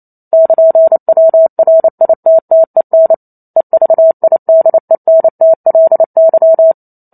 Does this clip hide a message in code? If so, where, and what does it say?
Yes: Morse code from 0.3 to 6.7 s, heard as a keyed tone, about 28 wpm, 650 Hz. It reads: <KN>WRITTEN EVIDENTLY